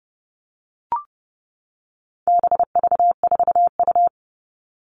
Morse code, heard as a keyed tone, about 30 wpm, 700 Hz. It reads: BV4U
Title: All Cut Numbers - 4-Digit 30wpm